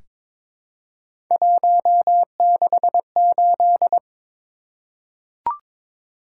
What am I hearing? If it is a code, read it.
Morse code, heard as a keyed tone, about 22 wpm, 700 Hz. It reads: 168
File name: practice / 22wpm / Three Letter-Number Combinations - 22wpm